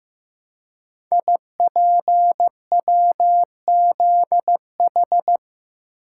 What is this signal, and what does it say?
Morse code, heard as a keyed tone, about 15 wpm, 700 Hz. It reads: IPWZH
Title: Code Groups - 5 Letters - 15wpm